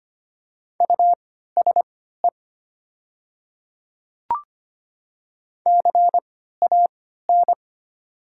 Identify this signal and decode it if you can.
Morse code, heard as a keyed tone, about 25 wpm, spaced as for 14 wpm, 700 Hz. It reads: USE CAN
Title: 3x Character Spacing - Top 100 Words - 25wpm